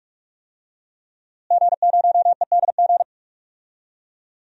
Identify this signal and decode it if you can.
Morse code, heard as a keyed tone, about 47 wpm, 700 Hz. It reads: G0EDG
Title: DXCC-100 Call Signs - 5 Characters - Repeated 2x - 45wpm